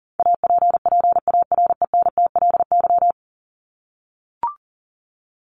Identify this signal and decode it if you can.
Morse code, heard as a keyed tone, about 40 wpm, 700 Hz. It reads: APPARENTLY
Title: LY Suffix - Top 100 Words with Base Words 40wpm